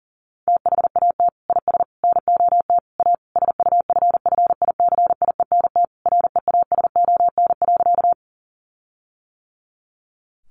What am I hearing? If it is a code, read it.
Morse code, heard as a keyed tone, about 40 wpm, 700 Hz. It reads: THAT IS NOT A SUFFICIENT REASON.
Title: Sentences from Top 1400 Words 40wpm